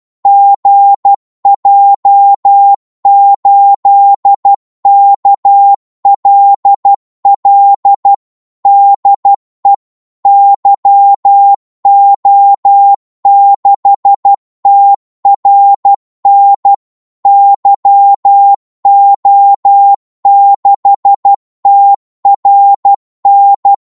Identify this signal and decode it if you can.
Morse code, heard as a keyed tone, about 12 wpm, 800 Hz. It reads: GJ8KLL DE YO6TRN YO6TRN
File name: Morse / qso-12wpm